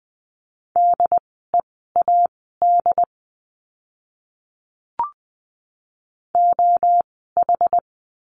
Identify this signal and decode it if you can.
Morse code, heard as a keyed tone, about 20 wpm, spaced as for 14 wpm, 700 Hz. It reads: DEAD OH
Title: 075 Lesson 13f - Letter L - Words 20wpm